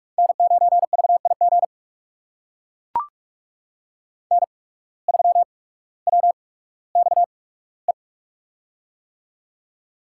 Morse code, heard as a keyed tone, about 42 wpm, spaced as for 19 wpm, 700 Hz. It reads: N9VIG N3WXE
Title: USA Call Signs - 5 Characters - 8x Character Spacing - 45wpm